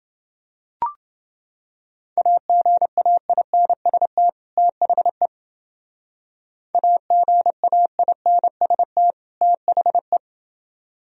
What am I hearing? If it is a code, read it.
Morse code, heard as a keyed tone, about 28 wpm, 700 Hz. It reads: AGAINST THE AGAINST THE